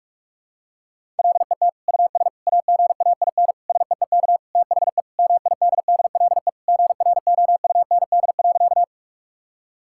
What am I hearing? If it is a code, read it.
Morse code, heard as a keyed tone, about 41 wpm, 700 Hz. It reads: LET US AGAIN SEEK THE MIDDLE GROUND.
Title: Sentences from Top 1600 Words 45wpm